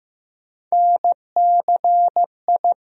Morse code, heard as a keyed tone, about 15 wpm, 700 Hz. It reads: NCI
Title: USA Call Sign Postfixes - 3 Letters 15wpm